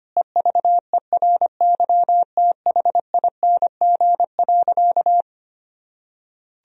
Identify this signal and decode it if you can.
Morse code, heard as a keyed tone, about 25 wpm, 700 Hz. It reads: EVERYTHING.
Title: Sentences from Top 1000 Words - Review Part 1 - 25wpm